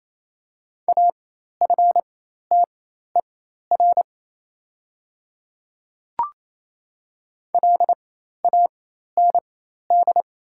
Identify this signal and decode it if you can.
Morse code, heard as a keyed tone, about 28 wpm, spaced as for 13 wpm, 700 Hz. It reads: AFTER LAND
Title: 4x Character Spacing - Top 300 Words - 28wpm